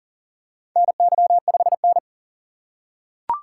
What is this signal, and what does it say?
Morse code, heard as a keyed tone, about 37 wpm, 700 Hz. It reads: NY5N
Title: USA Call Signs - 3x Character Spacing - 40wpm